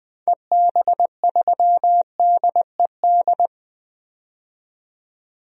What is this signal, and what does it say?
Morse code, heard as a keyed tone, about 20 wpm, 700 Hz. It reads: EB3DED
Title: DXCC-100 Call Signs - 6 Characters - 20wpm